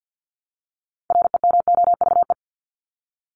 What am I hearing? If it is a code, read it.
Morse code, heard as a keyed tone, about 50 wpm, 700 Hz. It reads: REMOVE
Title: Top 751-1000 Words 50wpm